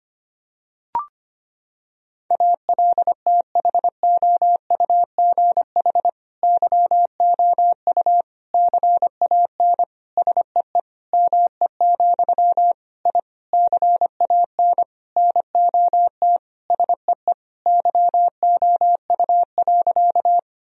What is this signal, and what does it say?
Morse code, heard as a keyed tone, about 25 wpm, 700 Hz. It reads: ALTHOUGH YOU CAN SEE ME, I CAN NOT SEE YOU.
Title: Sentences from Top 1000 Words - Review Part 3 - 25wpm